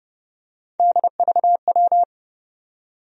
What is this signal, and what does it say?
Morse code, heard as a keyed tone, about 30 wpm, 700 Hz. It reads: DVW
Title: USA Call Sign Postfixes - 3 Letters 30wpm